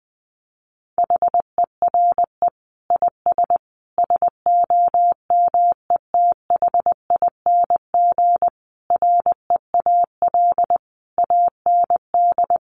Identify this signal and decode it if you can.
Morse code, heard as a keyed tone, about 20 wpm, 700 Hz. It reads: HERE IS SOMETHING REAL AND
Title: Sentences from Top 1500 Words 20wpm